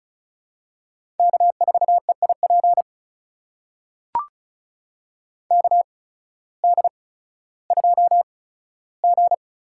Morse code, heard as a keyed tone, about 36 wpm, spaced as for 16 wpm, 700 Hz. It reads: K4EIP KD2G
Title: USA Call Signs - 8x Character Spacing - 35wpm